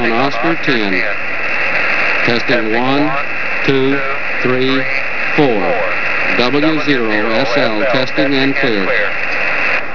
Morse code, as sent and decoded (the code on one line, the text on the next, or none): none